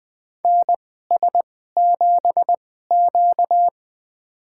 Morse code, heard as a keyed tone, about 20 wpm, spaced as for 14 wpm, 700 Hz. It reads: NS7Q
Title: USA Call Signs - 2x Character Spacing - 20wpm